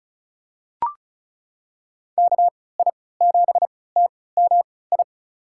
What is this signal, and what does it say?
Morse code, heard as a keyed tone, about 36 wpm, spaced as for 20 wpm, 700 Hz. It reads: KI7TMI
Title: USA Call Signs - 6 Characters - 3x Character Spacing - 35wpm